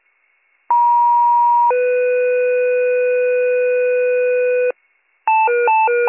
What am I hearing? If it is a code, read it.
Two-tone page, heard as a one-second tone, then a three-second tone, about 950 then 500 Hz, with noise. It is heard from 0.7 to 4.7 s.